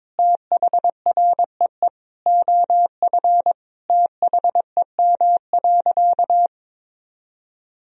Morse code, heard as a keyed tone, about 22 wpm, 700 Hz. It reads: THREE OF THEM.
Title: Sentences from Top 1000 Words - Review Part 1 - 22wpm